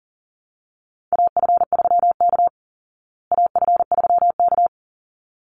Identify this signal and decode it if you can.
Morse code, heard as a keyed tone, about 40 wpm, 700 Hz. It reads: AF3K AF3K
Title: Call Signs Repeated 3x - 40wpm